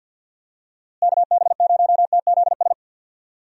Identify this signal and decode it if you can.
Morse code, heard as a keyed tone, about 50 wpm, 700 Hz. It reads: KB0TZS
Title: USA Call Signs - 6 Characters - Repeated 3x - 50wpm